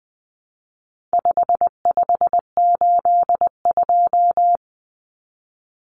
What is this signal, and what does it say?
Morse code, heard as a keyed tone, about 20 wpm, 700 Hz. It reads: 5582
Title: Numbers - 4-Digit 20wpm